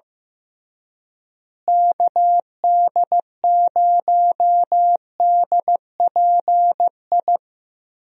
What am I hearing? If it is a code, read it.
Morse code, heard as a keyed tone, about 15 wpm, 700 Hz. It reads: KD0DPI
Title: USA Call Signs - 6 Characters - Repeated 2x - 15wpm